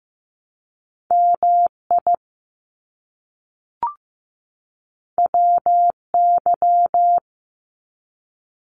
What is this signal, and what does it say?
Morse code, heard as a keyed tone, about 15 wpm, 700 Hz. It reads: MI WY
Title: US States 15wpm